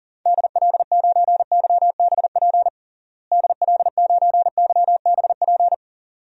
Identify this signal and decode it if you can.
Morse code, heard as a keyed tone, about 40 wpm, 700 Hz. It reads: DL9YBP DL9YBP
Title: DXCC-100 Call Signs - 6 Characters - Repeated 2x - 40wpm